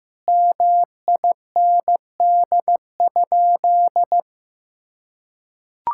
Morse code, heard as a keyed tone, about 16 wpm, 700 Hz. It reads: MIND?
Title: Sentences from Top 1000 Words - Review Part 5 - 15wpm